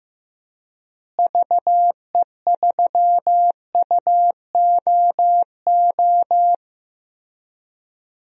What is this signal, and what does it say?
Morse code, heard as a keyed tone, about 15 wpm, 700 Hz. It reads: VE3UOO
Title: DXCC-100 Call Signs - 6 Characters - Repeated 3x - 15wpm